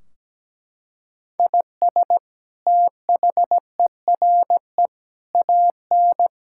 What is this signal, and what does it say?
Morse code, heard as a keyed tone, about 17 wpm, 700 Hz. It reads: IS THERE AN
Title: Sentences from Top 1600 Words 17wpm